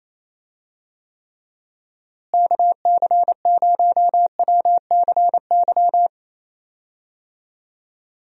Morse code, heard as a keyed tone, about 28 wpm, 700 Hz. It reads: KC0WCY
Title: USA Call Signs - 6 Characters - Repeated 3x - 28wpm